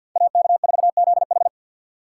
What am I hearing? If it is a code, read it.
Morse code, heard as a keyed tone, about 48 wpm, 700 Hz. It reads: AK4ZH
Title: USA Call Signs - 5 Characters - 4x Character Spacing - 50wpm